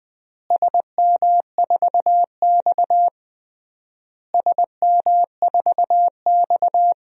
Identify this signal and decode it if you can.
Morse code, heard as a keyed tone, about 20 wpm, 700 Hz. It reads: SM4X SM4X
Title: DXCC-100 Call Signs - 4 Characters - Repeated 3x - 20wpm